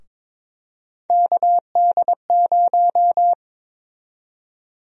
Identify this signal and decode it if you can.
Morse code, heard as a keyed tone, about 22 wpm, 700 Hz. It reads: KD0
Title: USA Call Sign Prefixes - 22wpm